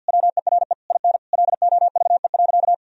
Morse code, heard as a keyed tone, about 47 wpm, 700 Hz. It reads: WERE IN LOVE.
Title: Sentences from Top 1000 Words - Review Part 4 - 50wpm